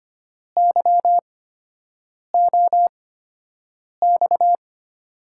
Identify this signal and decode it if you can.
Morse code, heard as a keyed tone, about 25 wpm, spaced as for 7 wpm, 700 Hz. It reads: YOX
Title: USA Call Signs - 5 Characters - 8x Character Spacing - 25wpm